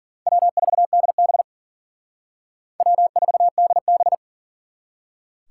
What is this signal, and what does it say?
Morse code, heard as a keyed tone, about 51 wpm, 700 Hz. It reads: W4DB W4DB